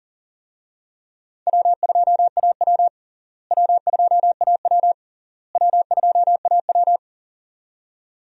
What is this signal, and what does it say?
Morse code, heard as a keyed tone, about 37 wpm, 700 Hz. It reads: W2AW W2AW W2AW
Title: USA Call Signs - 4 Characters - Repeated 3x - 40wpm